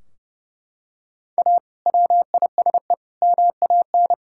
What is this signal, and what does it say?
Morse code, heard as a keyed tone, about 31 wpm, 700 Hz. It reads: A WISE MAN
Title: Sentences from Top 1600 Words 30wpm